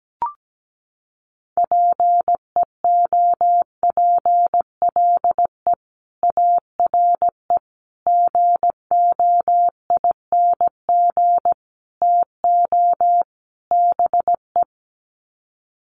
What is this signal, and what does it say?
Morse code, heard as a keyed tone, about 17 wpm, 700 Hz. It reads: PEOPLE ARE GOING TO BE
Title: Sets of 5 Words Encore 17wpm